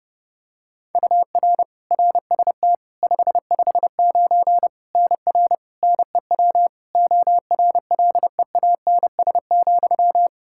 Morse code, heard as a keyed tone, about 30 wpm, 700 Hz. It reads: UR RST 559 NR NEW ORLEANS,
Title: K1USN SST QSO Practice 30wpm